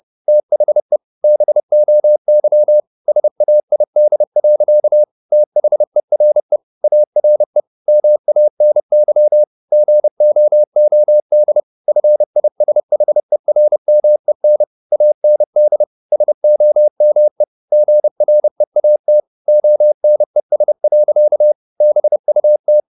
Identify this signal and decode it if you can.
Morse code, heard as a keyed tone, about 30 wpm, 600 Hz. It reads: THE BOY SAID. THERE ARE MANY GOOD FISHERMEN AND SOME GREAT ONES. BUT
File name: the old man and the sea / the old man and the sea 1 30wpm